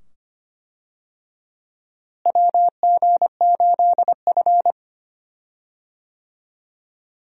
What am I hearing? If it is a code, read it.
Morse code, heard as a keyed tone, about 25 wpm, 700 Hz. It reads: WG8F
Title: USA Call Signs - 4 Characters - Repeated 2x - 25wpm